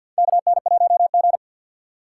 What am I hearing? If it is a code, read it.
Morse code, heard as a keyed tone, about 53 wpm, 700 Hz. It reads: KN1G
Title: USA Call Signs - 4 Characters - 3x Character Spacing - 50wpm